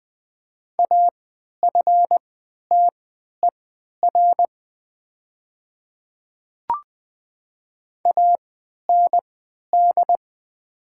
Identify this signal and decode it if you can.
Morse code, heard as a keyed tone, about 20 wpm, spaced as for 11 wpm, 700 Hz. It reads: AFTER AND
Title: 3x Character Spacing - Top 300 Words - 20wpm